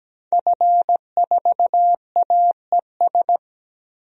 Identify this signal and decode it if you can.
Morse code, heard as a keyed tone, about 17 wpm, 700 Hz. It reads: F4AES
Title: DXCC-100 Call Signs - 5 Characters - Repeated 2x - 17wpm